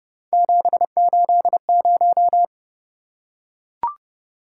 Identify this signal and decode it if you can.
Morse code, heard as a keyed tone, about 28 wpm, 700 Hz. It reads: 780